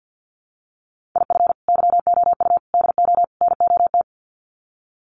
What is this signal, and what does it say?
Morse code, heard as a keyed tone, about 50 wpm, 700 Hz. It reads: IF YOU DO NOT
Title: Sets of 4 Words 50wpm